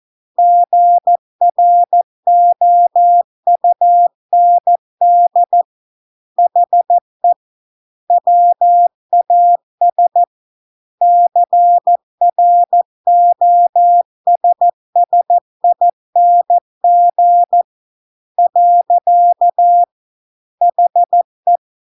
Morse code, heard as a keyed tone, about 14 wpm, 700 Hz. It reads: GROUND HE WAS CROSSING . HE